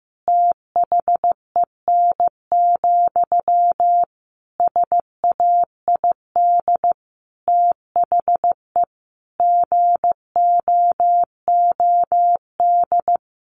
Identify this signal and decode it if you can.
Morse code, heard as a keyed tone, about 15 wpm, 700 Hz. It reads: THEN, SAID THE GOOD
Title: Sentences from Top 1100 Words 15wpm